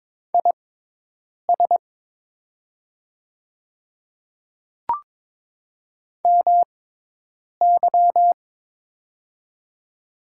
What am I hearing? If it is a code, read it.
Morse code, heard as a keyed tone, about 22 wpm, spaced as for 8 wpm, 700 Hz. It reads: IS MY
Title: 6x Character Spacing - Top 200 Words - 22wpm